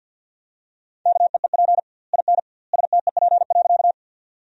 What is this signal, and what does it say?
Morse code, heard as a keyed tone, about 50 wpm, 700 Hz. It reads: KEEP IN STEP.
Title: 2x Word-Spacing - Sentences from Top 400 Words - 50wpm